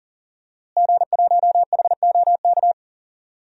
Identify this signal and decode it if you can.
Morse code, heard as a keyed tone, about 40 wpm, 700 Hz. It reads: G1HOK
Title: DXCC-100 Call Signs - 5 Characters - Repeated 3x - 40wpm